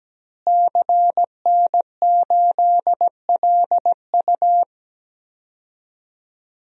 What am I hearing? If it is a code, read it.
Morse code, heard as a keyed tone, about 17 wpm, 700 Hz. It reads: CN8LU
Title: DXCC-100 Call Signs - 5 Characters - Repeated 2x - 17wpm